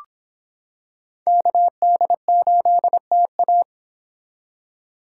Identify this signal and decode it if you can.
Morse code, heard as a keyed tone, about 26 wpm, 700 Hz. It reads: KD8TA